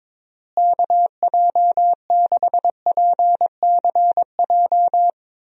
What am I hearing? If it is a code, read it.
Morse code, heard as a keyed tone, about 22 wpm, 700 Hz. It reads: KJ6PCJ
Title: USA Call Signs - 6 Characters - Repeated 2x - 22wpm